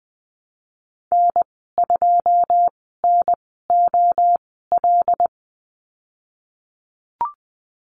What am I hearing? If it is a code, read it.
Morse code, heard as a keyed tone, about 20 wpm, spaced as for 15 wpm, 700 Hz. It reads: N2NOL
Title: 082 Lesson 14g - Number 2 - Call Signs 20wpm